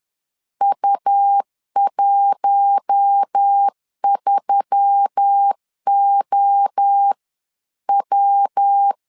Morse code, heard as a keyed tone, about 10 wpm, 800 Hz. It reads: U13OW
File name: LEZIONI / 4 Q1S3 10wpm 2427